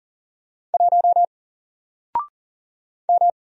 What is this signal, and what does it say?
Morse code, heard as a keyed tone, about 40 wpm, 700 Hz. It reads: 1 M
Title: Single Character (Letter:Number:Punctuation:Prosign) 40wpm